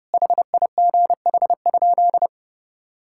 Morse code, heard as a keyed tone, about 30 wpm, 700 Hz. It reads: HIGH?
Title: Sentences from Top 1000 Words - Review Part 5 - 30wpm